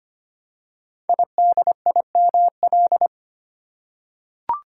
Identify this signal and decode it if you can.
Morse code, heard as a keyed tone, about 25 wpm, 700 Hz. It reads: IDIML
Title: Code Groups - 5 Letter:Numbers - 25wpm